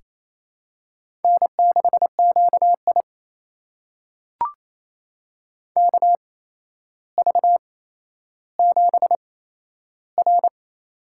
Morse code, heard as a keyed tone, about 27 wpm, spaced as for 12 wpm, 700 Hz. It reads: N6QI KV7R
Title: USA Call Signs - 4 Characters - 8x Character Spacing - 28wpm